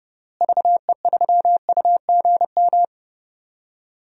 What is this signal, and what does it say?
Morse code, heard as a keyed tone, about 31 wpm, 700 Hz. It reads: VE3UGM
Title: DXCC-100 Call Signs - 6 Characters - Repeated 2x - 30wpm